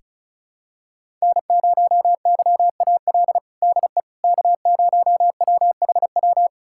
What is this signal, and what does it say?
Morse code, heard as a keyed tone, about 36 wpm, 700 Hz. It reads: N0YAL DE K0WHW
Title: USA Callsign DE Callsign - 5 Characters - 35wpm